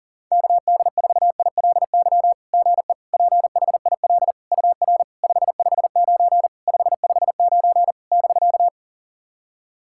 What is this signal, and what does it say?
Morse code, heard as a keyed tone, about 40 wpm, 700 Hz. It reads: KD4ILY GE PHIL UR 559 559 <BK>
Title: SOTA QSO Practice 40wpm